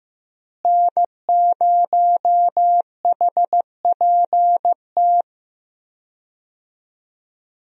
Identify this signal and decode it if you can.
Morse code, heard as a keyed tone, about 16 wpm, 700 Hz. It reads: N0HPT
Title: USA Call Signs - 5 Characters - Repeated 3x - 15wpm